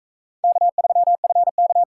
Morse code, heard as a keyed tone, about 44 wpm, 700 Hz. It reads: K3FK